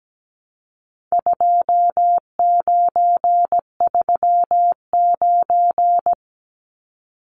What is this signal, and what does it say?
Morse code, heard as a keyed tone, about 17 wpm, 700 Hz. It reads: 2939
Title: Numbers - 4-Digit 17wpm